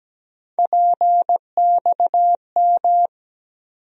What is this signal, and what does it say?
Morse code, heard as a keyed tone, about 17 wpm, 700 Hz. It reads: PXM
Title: USA Call Sign Postfixes - 3 Letters 17wpm